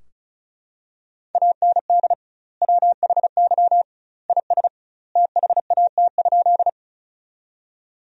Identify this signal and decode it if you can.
Morse code, heard as a keyed tone, about 36 wpm, 700 Hz. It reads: AND WHY IS THAT?
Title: 2x Word-Spacing - Sentences from Top 200 Words - 35wpm